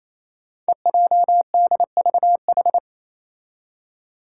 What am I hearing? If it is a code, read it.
Morse code, heard as a keyed tone, about 28 wpm, 700 Hz. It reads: EJDVH
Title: Five Challenging Letter-Number Combinations 28wpm